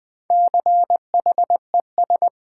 Morse code, heard as a keyed tone, about 20 wpm, 700 Hz. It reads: CHES
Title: Build A Word - 1 Letter At A Time - 5-Letter Words - 20wpm